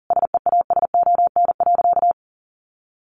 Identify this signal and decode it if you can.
Morse code, heard as a keyed tone, about 40 wpm, 700 Hz. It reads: SEASON.
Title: Sentences from Top 1400 Words 40wpm